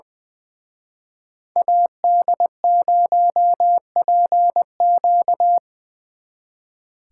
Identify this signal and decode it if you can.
Morse code, heard as a keyed tone, about 24 wpm, 700 Hz. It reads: AD0PQ
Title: Speed Racing - USA Call Signs 20wpm